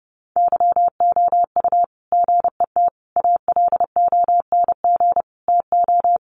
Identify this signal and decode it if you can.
Morse code, heard as a keyed tone, about 30 wpm, 700 Hz. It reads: YOU GET ALONG TO
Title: Sentences from Top 700 Words 30wpm